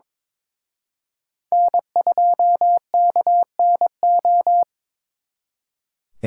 Morse code, heard as a keyed tone, about 26 wpm, 700 Hz. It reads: N2KNO
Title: Speed Racing - USA 1x3 Call Signs 22wpm